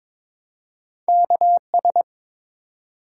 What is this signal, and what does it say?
Morse code, heard as a keyed tone, about 22 wpm, 700 Hz. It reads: KS
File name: ARRL Sections v2 - 22wpm